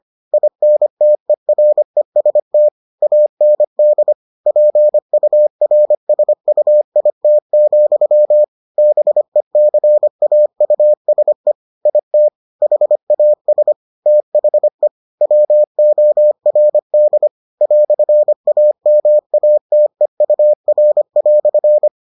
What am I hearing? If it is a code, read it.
Morse code, heard as a keyed tone, about 25 wpm, 600 Hz. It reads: INTEREST AND PURSUIT, BECAUSE IT HAS THE WORD "AMATEUR"